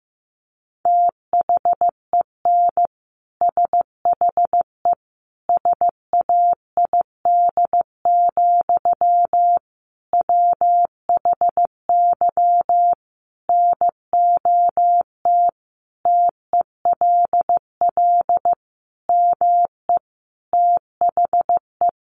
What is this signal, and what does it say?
Morse code, heard as a keyed tone, about 15 wpm, 700 Hz. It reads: THEN SHE SAID, WHY NOT TELL ME THE
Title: Sentences from Top 800 Words 15wpm